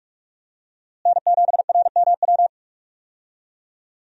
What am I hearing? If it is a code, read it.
Morse code, heard as a keyed tone, about 46 wpm, 700 Hz. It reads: N7RMW
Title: USA Call Signs - 5 Characters - Repeated 3x - 45wpm